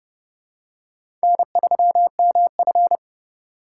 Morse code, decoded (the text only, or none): N3MF